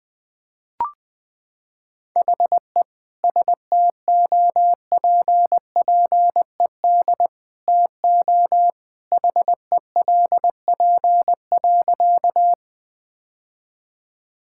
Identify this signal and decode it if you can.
Morse code, heard as a keyed tone, about 20 wpm, 700 Hz. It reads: HE STOPPED TO HELP.
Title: Sentences with 2-Word Phrases - 20wpm